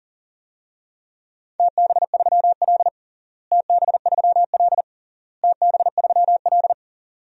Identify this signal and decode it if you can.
Morse code, heard as a keyed tone, about 40 wpm, 700 Hz. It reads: TB3L TB3L TB3L
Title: DXCC-100 Call Signs - 4 Characters - Repeated 3x - 40wpm